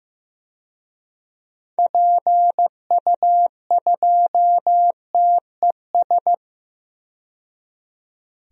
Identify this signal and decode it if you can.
Morse code, heard as a keyed tone, about 15 wpm, 700 Hz. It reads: PU2TES
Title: DXCC-100 Call Signs - 6 Characters - 15wpm